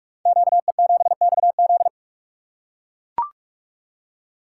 Morse code, heard as a keyed tone, about 44 wpm, 700 Hz. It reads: QE7XZ
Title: Code Groups - 5 Letter:Numbers - 45wpm